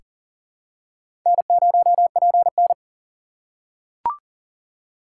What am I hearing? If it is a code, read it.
Morse code, heard as a keyed tone, about 40 wpm, 690 Hz. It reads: N0PN
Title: USA Call Signs - 2x Character Spacing - 40wpm